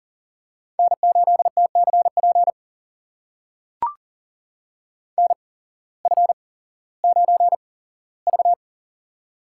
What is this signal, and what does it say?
Morse code, heard as a keyed tone, about 40 wpm, spaced as for 19 wpm, 700 Hz. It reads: N8TCP NF9V
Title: USA Call Signs - 8x Character Spacing - 40wpm